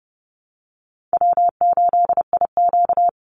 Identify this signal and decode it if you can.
Morse code, heard as a keyed tone, about 30 wpm, 700 Hz. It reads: W8IQ
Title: Easy Call Signs - 30wpm